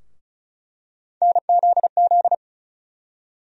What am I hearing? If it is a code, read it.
Morse code, heard as a keyed tone, about 36 wpm, 700 Hz. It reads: NZZ
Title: USA Call Sign Postfixes - 3 Letters 35wpm